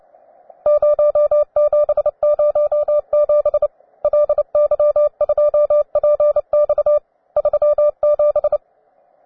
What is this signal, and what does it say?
Morse code, heard as a keyed tone, about 29 wpm, 590 Hz. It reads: 0707 LY2PX 37
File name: wae / cw beginner / ua9am cw